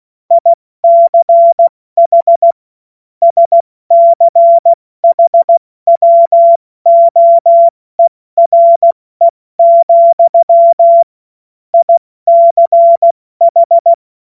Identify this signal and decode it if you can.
Morse code, heard as a keyed tone, about 16 wpm, spaced as for 14 wpm, 680 Hz. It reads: ICH SCHWOERE, ICH
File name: Morsekurs / Witze / Witz 20. September 2022 70 F80